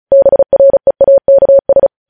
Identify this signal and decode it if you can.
Morse code, heard as a keyed tone, about 36 wpm, 550 Hz. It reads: BREAKS
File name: podcast / News headlines-1x-35x35